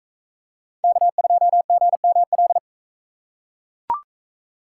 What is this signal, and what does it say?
Morse code, heard as a keyed tone, about 42 wpm, 700 Hz. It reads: K2GML